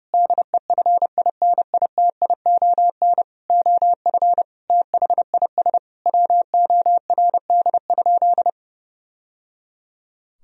Morse code, heard as a keyed tone, about 30 wpm, 700 Hz. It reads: DEFINITION OF THIS WORD?
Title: Sentences from Top 1000 Words - Review Part 4 - 30wpm